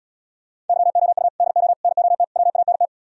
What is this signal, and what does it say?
Morse code, heard as a keyed tone, about 77 wpm, 700 Hz. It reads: YOU DO NOT WANT